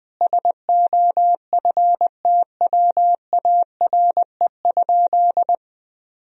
Morse code, heard as a keyed tone, about 21 wpm, 700 Hz. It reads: SOFTWARE?
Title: Sentences from Top 1000 Words - Review Part 3 - 20wpm